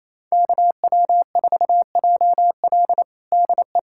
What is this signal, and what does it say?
Morse code, heard as a keyed tone, about 28 wpm, 700 Hz. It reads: KW4JL DE
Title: USA Callsign DE Callsign - 5 Characters - 28wpm